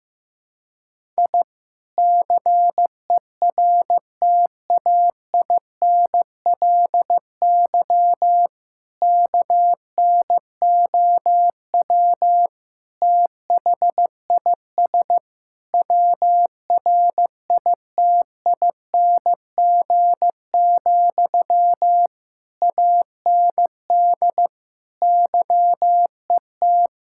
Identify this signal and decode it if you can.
Morse code, heard as a keyed tone, about 14 wpm, 700 Hz. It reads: I CERTAINLY KNOW THIS WRITING, AND YET
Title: Sentences from Top 1000 Words - Review Part 6 - 15wpm